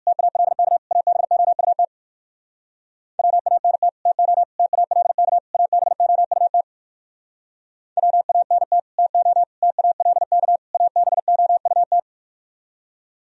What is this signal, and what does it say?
Morse code, heard as a keyed tone, about 57 wpm, 700 Hz. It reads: TALK ABOUT WANT TO TALK ABOUT WANT TO TALK ABOUT